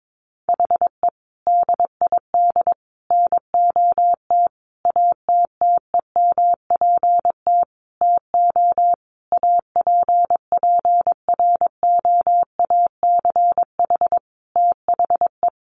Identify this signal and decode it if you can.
Morse code, heard as a keyed tone, about 22 wpm, 700 Hz. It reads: HE DID NOT ATTEMPT TO APPROACH THE
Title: Sentences from Top 1400 Words 22wpm